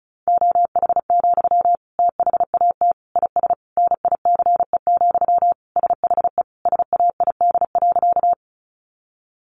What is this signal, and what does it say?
Morse code, heard as a keyed tone, about 34 wpm, 700 Hz. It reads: OH, THAT IS NICE, SHE SAID.
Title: Sentences from Top 1100 Words 35wpm